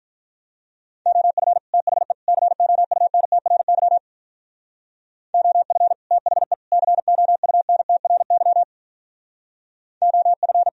Morse code, heard as a keyed tone, about 42 wpm, 700 Hz. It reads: OF THE COUNTRY OF THE COUNTRY OF
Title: Speed-Racing - Sets of 3 Words 35wpm